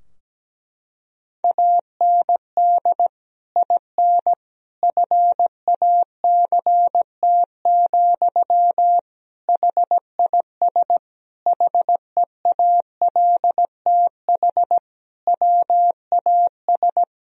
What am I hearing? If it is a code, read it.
Morse code, heard as a keyed tone, about 17 wpm, 700 Hz. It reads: AND IN FACT, HIS HEALTH WAS
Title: Sentences from Top 1000 Words - Review Part 3 - 17wpm